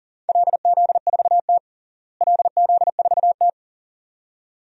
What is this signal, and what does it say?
Morse code, heard as a keyed tone, about 41 wpm, 700 Hz. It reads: LZ4T LZ4T
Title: DXCC-100 Call Signs - 4 Characters - Repeated 2x - 40wpm